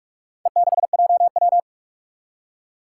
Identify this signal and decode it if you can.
Morse code, heard as a keyed tone, about 46 wpm, 700 Hz. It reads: E6JW
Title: Four Challenging Letter-Number Combinations 45wpm